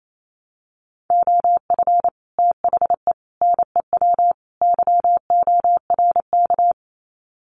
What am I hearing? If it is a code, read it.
Morse code, heard as a keyed tone, about 28 wpm, 700 Hz. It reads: OF THE NEW YORK
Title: Sets of 4 Words 28wpm